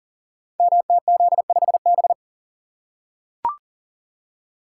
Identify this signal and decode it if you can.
Morse code, heard as a keyed tone, about 37 wpm, 700 Hz. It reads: MTZ5B
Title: Code Groups - 5 Letter:Numbers - 40wpm